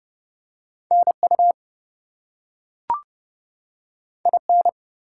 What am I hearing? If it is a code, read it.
Morse code, heard as a keyed tone, about 30 wpm, 700 Hz. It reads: NU IN
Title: US States and Canadian Provinces 30wpm